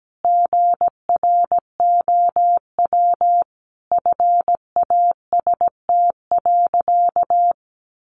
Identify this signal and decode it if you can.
Morse code, heard as a keyed tone, about 17 wpm, 700 Hz. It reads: GROW FAST.
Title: Sentences from Top 1300 Words 17wpm